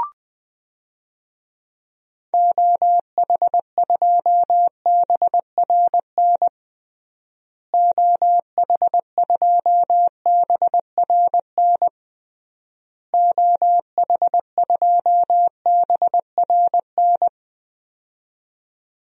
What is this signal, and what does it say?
Morse code, heard as a keyed tone, about 20 wpm, 700 Hz. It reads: OH2BRN OH2BRN OH2BRN
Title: DXCC-100 Call Signs - 6 Characters - Repeated 3x - 20wpm